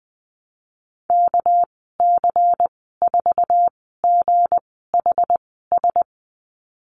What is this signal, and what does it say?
Morse code, heard as a keyed tone, about 20 wpm, spaced as for 14 wpm, 700 Hz. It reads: KC4GHS
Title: 213 Lesson 34g - Number 8 - Call Signs 20wpm